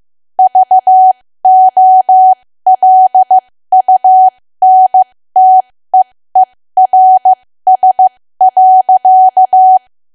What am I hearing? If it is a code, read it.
Morse code, heard as a keyed tone, about 15 wpm, spaced as for 13 wpm, 750 Hz. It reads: VOLUNTEERS.
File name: cwa / W1AW-Level2 / 13WPM